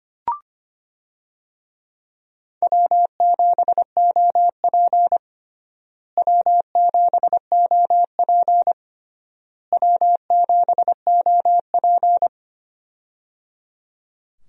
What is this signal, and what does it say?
Morse code, heard as a keyed tone, about 25 wpm, 700 Hz. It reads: W7OP W7OP W7OP
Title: USA Call Signs - 4 Characters - Repeated 3x - 25wpm